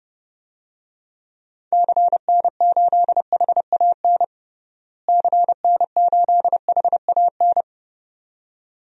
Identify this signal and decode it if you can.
Morse code, heard as a keyed tone, about 32 wpm, 700 Hz. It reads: CN8HAN CN8HAN
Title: DXCC-100 Call Signs - 6 Characters - Repeated 2x - 30wpm